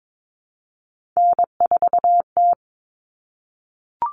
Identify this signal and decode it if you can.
Morse code, heard as a keyed tone, about 22 wpm, 700 Hz. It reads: N4T
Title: All Cut Numbers - 3-Digit 22wpm